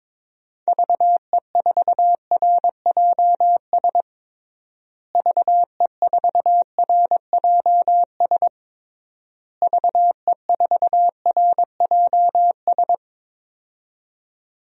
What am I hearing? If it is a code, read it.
Morse code, heard as a keyed tone, about 22 wpm, 700 Hz. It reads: VE4RJS VE4RJS VE4RJS
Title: DXCC-100 Call Signs - 6 Characters - Repeated 3x - 22wpm